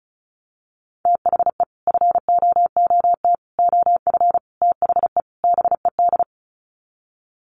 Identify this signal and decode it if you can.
Morse code, heard as a keyed tone, about 34 wpm, 700 Hz. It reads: THE FOOT OF THE BED
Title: Sets of 5 Words Encore2 35wpm